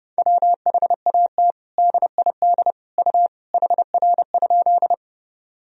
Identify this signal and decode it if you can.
Morse code, heard as a keyed tone, about 30 wpm, 700 Hz. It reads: WHAT DID U HR?
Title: Sentences from Top 300 Words and Abbreviations 30wpm